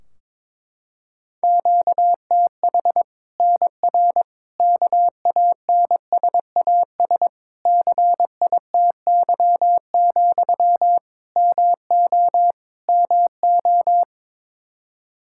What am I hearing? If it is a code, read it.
Morse code, heard as a keyed tone, about 22 wpm, 700 Hz. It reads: QTH NR KANSAS CITY, MO MO
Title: Practice QSO - Exchange 1 - Beginning - 22wpm